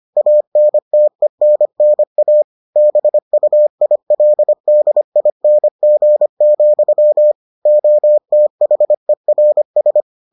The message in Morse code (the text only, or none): ANTENNA BUILDING, OTHERS